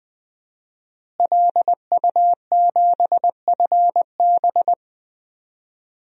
Morse code, heard as a keyed tone, about 20 wpm, 700 Hz. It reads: LU7FB
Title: DXCC-100 Call Signs - 5 Characters - Repeated 3x - 20wpm